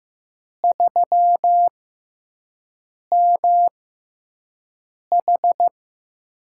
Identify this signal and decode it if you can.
Morse code, heard as a keyed tone, about 15 wpm, spaced as for 5 wpm, 700 Hz. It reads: 3MH
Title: USA Call Signs - 4 Characters - 6x Character Spacing - 15wpm